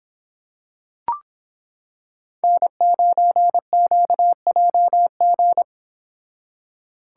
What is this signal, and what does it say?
Morse code, heard as a keyed tone, about 26 wpm, 700 Hz. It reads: N9QJG